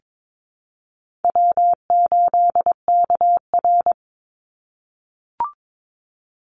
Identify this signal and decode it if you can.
Morse code, heard as a keyed tone, about 22 wpm, 700 Hz. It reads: W8KR
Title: Easy Call Signs - 22wpm